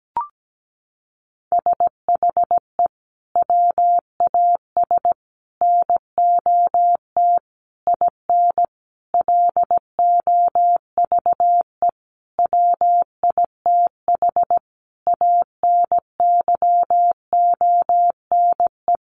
Morse code, heard as a keyed tone, about 17 wpm, 700 Hz. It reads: SHE WAS NOT IN LOVE WITH ANYONE
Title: Sentences from Top 900 Words 17wpm